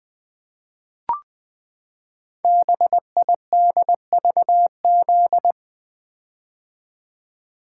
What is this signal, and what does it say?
Morse code, heard as a keyed tone, about 20 wpm, 700 Hz. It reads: BIDVZ
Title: Code Groups - 5 Letter:Numbers - 20wpm